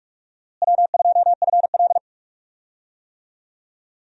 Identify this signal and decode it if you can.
Morse code, heard as a keyed tone, about 46 wpm, 700 Hz. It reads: W2FL
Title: USA Call Signs - 4 Characters - Repeated 2x - 45wpm